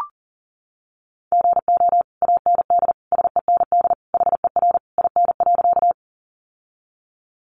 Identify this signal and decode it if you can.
Morse code, heard as a keyed tone, about 40 wpm, 700 Hz. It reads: GO AND SEND HER IN.
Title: Sentences from Top 1500 Words 40wpm